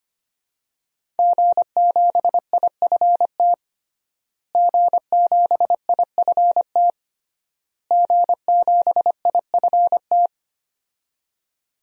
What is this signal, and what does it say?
Morse code, heard as a keyed tone, about 25 wpm, 700 Hz. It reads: G7IFT G7IFT G7IFT
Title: DXCC-100 Call Signs - 5 Characters - Repeated 3x - 25wpm